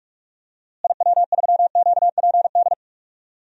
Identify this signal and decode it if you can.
Morse code, heard as a keyed tone, about 46 wpm, 700 Hz. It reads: IW3QPD
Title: DXCC-100 Call Signs - 6 Characters - Repeated 3x - 45wpm